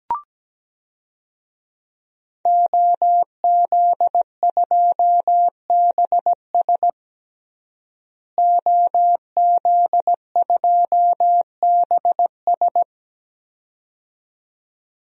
Morse code, heard as a keyed tone, about 17 wpm, 700 Hz. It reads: OZ2BS OZ2BS
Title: DXCC-100 Call Signs - 5 Characters - Repeated 2x - 17wpm